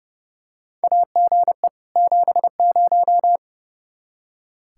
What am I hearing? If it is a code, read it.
Morse code, heard as a keyed tone, about 30 wpm, 700 Hz. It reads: AGE 70
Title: Practice QSO - Exchange 3 - More Info - 30wpm